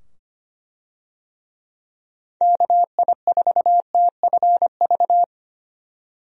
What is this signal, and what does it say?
Morse code, heard as a keyed tone, about 25 wpm, 700 Hz. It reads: KI4TFV
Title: USA Call Signs - 6 Characters - 25wpm